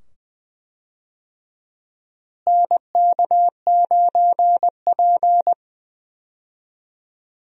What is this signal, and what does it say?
Morse code, heard as a keyed tone, about 20 wpm, 700 Hz. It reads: NK9P
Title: USA Call Signs - 4 Characters - Repeated 2x - 20wpm